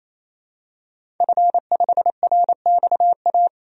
Morse code, heard as a keyed tone, about 28 wpm, 700 Hz. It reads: F5RXA
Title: DXCC-100 Call Signs - 5 Characters - 28wpm